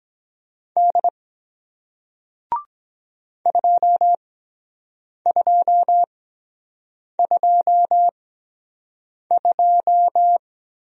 Morse code, heard as a keyed tone, about 23 wpm, spaced as for 7 wpm, 700 Hz. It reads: D 2222